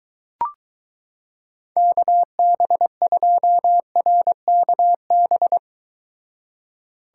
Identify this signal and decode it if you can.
Morse code, heard as a keyed tone, about 23 wpm, 700 Hz. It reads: KB2RKB